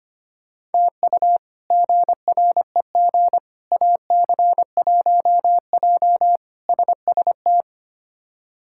Morse code, heard as a keyed tone, about 25 wpm, 700 Hz. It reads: TU GREG AC1J SST
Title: K1USN SST QSO Practice - 25wpm